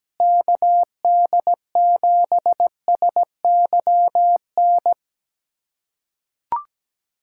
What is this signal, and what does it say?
Morse code, heard as a keyed tone, about 17 wpm, 700 Hz. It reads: KD7SYN
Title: USA Callsign DE Callsign 17wpm